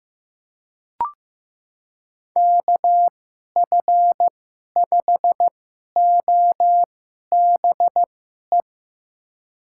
Morse code, heard as a keyed tone, about 15 wpm, spaced as for 11 wpm, 700 Hz. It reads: KF5OBE
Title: USA Call Signs - 6 Characters - 2x Character Spacing - 15wpm